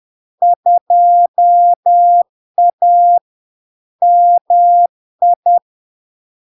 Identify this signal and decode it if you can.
Morse code, heard as a keyed tone, about 10 wpm, 700 Hz. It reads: 2A MI
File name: LARK-FD-2022-Log-in-CW-10wpm-700-Hz